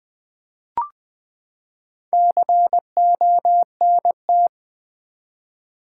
Morse code, heard as a keyed tone, about 20 wpm, 700 Hz. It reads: CONT
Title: Build A Word - 4 Letters At A Time - 8-Letter Words - 20wpm